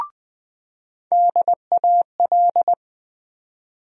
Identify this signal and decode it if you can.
Morse code, heard as a keyed tone, about 20 wpm, 700 Hz. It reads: DAL
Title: USA Call Sign Postfixes - 3 Letters 20wpm